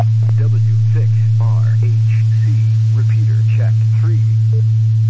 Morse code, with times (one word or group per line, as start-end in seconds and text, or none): none